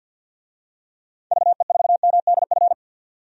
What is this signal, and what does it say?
Morse code, heard as a keyed tone, about 52 wpm, 700 Hz. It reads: VE4MDF
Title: DXCC-100 Call Signs - 6 Characters - 50wpm